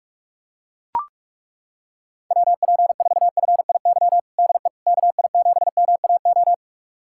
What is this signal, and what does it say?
Morse code, heard as a keyed tone, about 42 wpm, 700 Hz. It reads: WP4FIY DE KI7MAO
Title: USA Callsign DE Callsign 45wpm